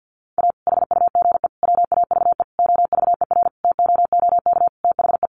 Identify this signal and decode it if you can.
Morse code, heard as a keyed tone, about 50 wpm, 700 Hz. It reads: A HUGE WAVE OVERTOOK THE
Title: Sentences from Top 1400 Words 50wpm